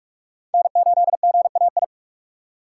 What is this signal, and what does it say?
Morse code, heard as a keyed tone, about 45 wpm, 700 Hz. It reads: N8GAI
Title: USA Call Signs - 2x Character Spacing - 45wpm